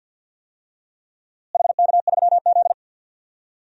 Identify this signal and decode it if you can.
Morse code, heard as a keyed tone, about 52 wpm, 700 Hz. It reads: HK3Z
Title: DXCC-100 Call Signs - 4 Characters - Repeated 2x - 50wpm